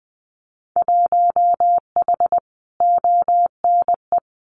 Morse code, heard as a keyed tone, about 20 wpm, 700 Hz. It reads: 1H ONE
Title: Winter Field Day Exchanges Encore 20wpm